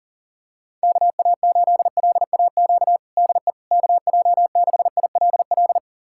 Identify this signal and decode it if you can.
Morse code, heard as a keyed tone, about 40 wpm, 700 Hz. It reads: KA8LAQ DE KJ6ILL
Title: USA Callsign DE Callsign 40wpm